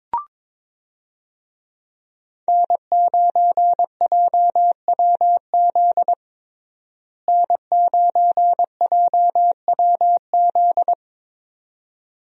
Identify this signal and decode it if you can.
Morse code, heard as a keyed tone, about 22 wpm, 700 Hz. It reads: N9JWZ N9JWZ
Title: USA Call Signs - 5 Characters - Repeated 2x - 22wpm